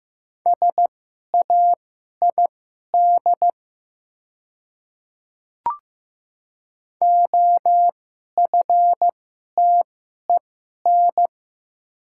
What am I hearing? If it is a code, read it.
Morse code, heard as a keyed tone, about 15 wpm, spaced as for 11 wpm, 700 Hz. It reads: SAID OFTEN
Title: 2x Character Spacing - Top 300 Words - 15wpm